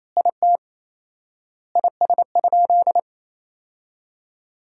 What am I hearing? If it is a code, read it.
Morse code, heard as a keyed tone, about 27 wpm, 700 Hz. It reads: IT IS?
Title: 4x Word-Spacing - Sentences from Top 100 Words - 28wpm